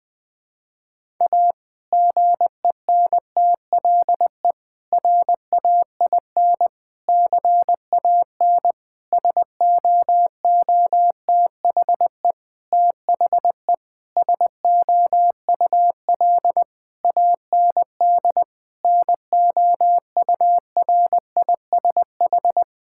Morse code, heard as a keyed tone, about 20 wpm, 700 Hz. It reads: A GENTLE RAIN CAN SOOTHE THE SOUL AND NOURISH
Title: Difficult Sentences with Hints - 20wpm